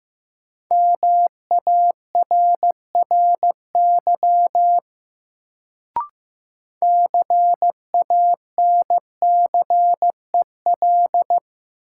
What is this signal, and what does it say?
Morse code, heard as a keyed tone, about 15 wpm, 700 Hz. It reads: MARRY CANCEL
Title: Sentences from Top 1200 Words - New Word Drill - 15wpm